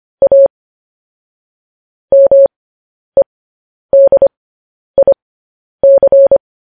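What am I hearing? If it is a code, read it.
Morse code, heard as a keyed tone, about 25 wpm, spaced as for 10 wpm, 550 Hz. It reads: A MEDIC
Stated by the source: Morse code news headlines sent at 10 wpm.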